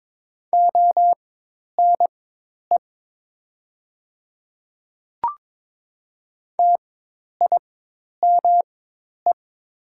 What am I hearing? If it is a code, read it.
Morse code, heard as a keyed tone, about 22 wpm, spaced as for 10 wpm, 700 Hz. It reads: ONE TIME
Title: 4x Character Spacing - Top 100 Words - 22wpm